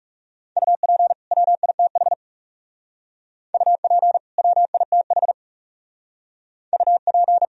Morse code, heard as a keyed tone, about 36 wpm, 700 Hz. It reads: UP WITH UP WITH UP
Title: Speed-Racing - Sets of 2 Words 30wpm